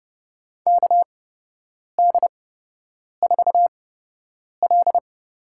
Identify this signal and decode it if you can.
Morse code, heard as a keyed tone, about 30 wpm, spaced as for 8 wpm, 700 Hz. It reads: KD4L
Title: USA Call Signs - 8x Character Spacing - 30wpm